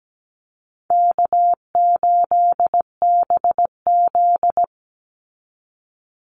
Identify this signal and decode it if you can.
Morse code, heard as a keyed tone, about 17 wpm, 700 Hz. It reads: K8BZ
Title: Easy Call Signs Repeated 2x - 17wpm